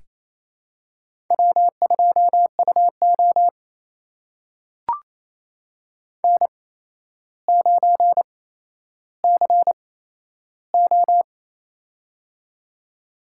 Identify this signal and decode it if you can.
Morse code, heard as a keyed tone, about 27 wpm, spaced as for 12 wpm, 700 Hz. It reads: W2UO N9CO
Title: USA Call Signs - 4 Characters - 8x Character Spacing - 28wpm